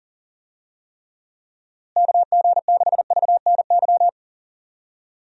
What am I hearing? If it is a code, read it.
Morse code, heard as a keyed tone, about 37 wpm, 700 Hz. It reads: KG6VNY
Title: USA Call Signs - 6 Characters - Repeated 3x - 40wpm